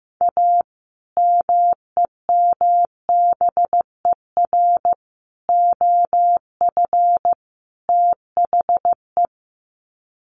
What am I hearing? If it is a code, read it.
Morse code, heard as a keyed tone, about 15 wpm, 700 Hz. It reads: A MEMBER OF THE
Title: Sets of 4 Words 15wpm